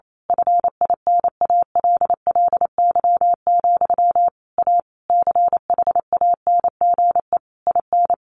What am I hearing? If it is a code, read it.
Morse code, heard as a keyed tone, about 28 wpm, 700 Hz. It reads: FINALLY, A CHANGE IN
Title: A Sound Of Thunder 28wpm